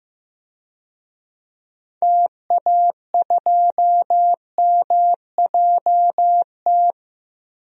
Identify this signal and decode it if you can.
Morse code, heard as a keyed tone, about 15 wpm, 700 Hz. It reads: TA2MJT
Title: DXCC-100 Call Signs - 6 Characters - 15wpm